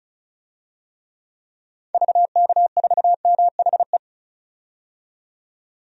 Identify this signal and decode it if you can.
Morse code, heard as a keyed tone, about 36 wpm, 690 Hz. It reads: VK4MHE
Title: DXCC-100 Call Signs - 6 Characters - 35wpm